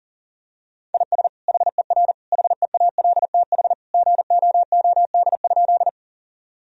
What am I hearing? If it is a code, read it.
Morse code, heard as a keyed tone, about 40 wpm, 700 Hz. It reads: IS HER HEALTH GOOD?
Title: Sentences from Top 1000 Words - Review Part 3 - 40wpm